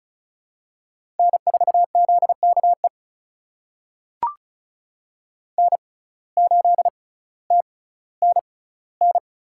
Morse code, heard as a keyed tone, about 36 wpm, spaced as for 18 wpm, 700 Hz. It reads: N4ZKE N8TNN
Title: USA Call Signs - 5 Characters - 6x Character Spacing - 35wpm